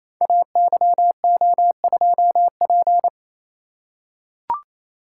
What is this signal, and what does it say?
Morse code, heard as a keyed tone, about 28 wpm, 700 Hz. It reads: AYO2P
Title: Code Groups - 5 Letter:Numbers - 28wpm